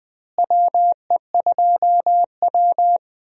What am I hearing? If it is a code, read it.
Morse code, heard as a keyed tone, about 20 wpm, 700 Hz. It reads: WE2W
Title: USA Call Signs - 4 Characters - Repeated 2x - 20wpm